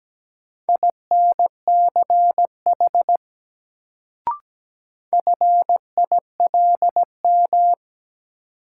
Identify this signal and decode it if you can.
Morse code, heard as a keyed tone, about 17 wpm, 700 Hz. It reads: INCH FILM
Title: Sentences from Top 1000 Words - Review Part 3 - New Word Drill - 17wpm